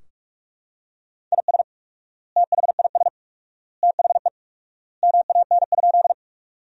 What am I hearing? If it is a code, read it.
Morse code, heard as a keyed tone, about 42 wpm, 700 Hz. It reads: IS THIS THE MAN?
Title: 4x Word-Spacing - Sentences from Top 200 Words - 45wpm